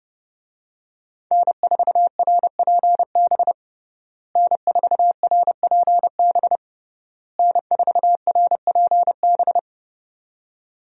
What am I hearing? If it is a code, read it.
Morse code, heard as a keyed tone, about 30 wpm, 700 Hz. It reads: N4RPB N4RPB N4RPB
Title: USA Call Signs - 5 Characters - Repeated 3x - 30wpm